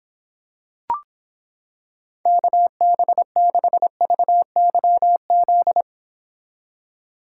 Morse code, heard as a keyed tone, about 26 wpm, 700 Hz. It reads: KB6VYZ